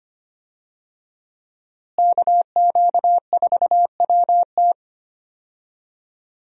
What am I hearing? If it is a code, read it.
Morse code, heard as a keyed tone, about 25 wpm, 700 Hz. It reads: KQ4WT
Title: USA Call Signs - 5 Characters - Repeated 3x - 25wpm